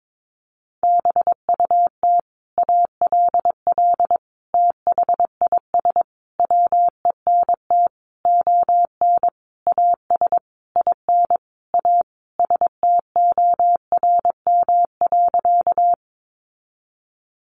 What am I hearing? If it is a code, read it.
Morse code, heard as a keyed tone, about 22 wpm, 700 Hz. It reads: BUT ALL THIS WENT ON AS IN A STORM.
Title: Sentences from Top 1500 Words 22wpm